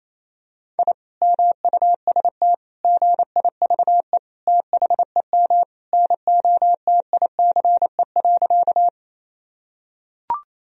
Morse code, heard as a keyed tone, about 28 wpm, 700 Hz. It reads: I MUST GIVE THEM NOTICE.
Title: Sentences from Top 1000 Words - Review Part 2 - 28wpm